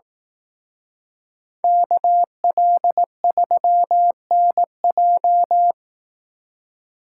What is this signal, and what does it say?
Morse code, heard as a keyed tone, about 18 wpm, 700 Hz. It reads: KL3NJ
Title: Speed Racing - USA 2x2 Call Signs 15wpm